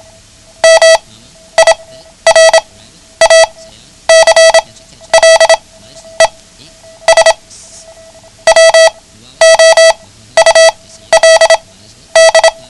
Morse code, heard as a keyed tone, about 27 wpm, spaced as for 12 wpm, 650 Hz. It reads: MIRACLES WOULD
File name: ham radio / CodeQuick / Code Quick 09